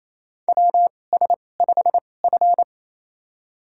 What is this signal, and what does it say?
Morse code, heard as a keyed tone, about 28 wpm, spaced as for 20 wpm, 700 Hz. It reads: WS5F
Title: USA Call Signs - 4 Characters - 2x Character Spacing - 28wpm